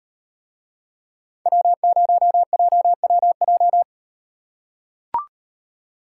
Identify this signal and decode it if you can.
Morse code, heard as a keyed tone, about 38 wpm, 700 Hz. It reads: W0JWJ